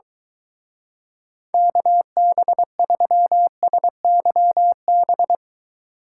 Morse code, heard as a keyed tone, about 23 wpm, 700 Hz. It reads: KB3SYB